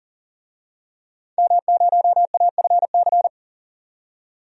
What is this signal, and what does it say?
Morse code, heard as a keyed tone, about 40 wpm, 700 Hz. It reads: M0AFC
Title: DXCC-100 Call Signs - 5 Characters - 40wpm